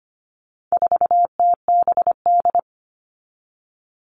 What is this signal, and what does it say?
Morse code, heard as a keyed tone, about 25 wpm, 700 Hz. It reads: 4TBD
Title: All Cut Numbers - 4-Digit 25wpm